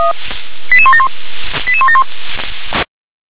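Five-tone selective call (ZVEI), heard as quick runs of five tones, tones of about 70 ms. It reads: 88171 88171
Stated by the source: ZVEI five-tone selective call received on an SDR.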